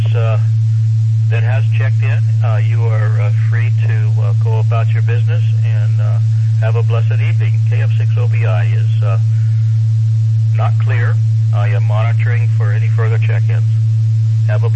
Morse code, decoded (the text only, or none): none